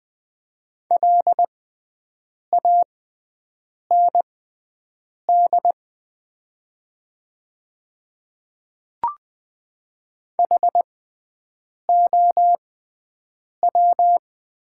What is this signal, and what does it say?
Morse code, heard as a keyed tone, about 20 wpm, spaced as for 7 wpm, 700 Hz. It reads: LAND HOW
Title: 6x Character Spacing - Top 200 Words - 20wpm